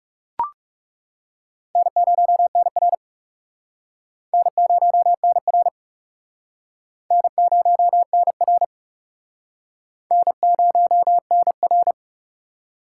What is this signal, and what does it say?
Morse code, heard as a keyed tone, about 38 wpm, 700 Hz. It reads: N0NR N0NR N0NR N0NR